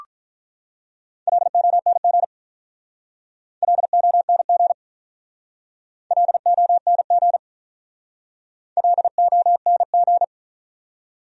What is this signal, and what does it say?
Morse code, heard as a keyed tone, about 45 wpm, 700 Hz. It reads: LONG LONG LONG LONG